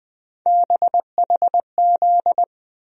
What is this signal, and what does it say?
Morse code, heard as a keyed tone, about 20 wpm, 700 Hz. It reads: BHZ
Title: USA Call Sign Postfixes - 3 Letters 20wpm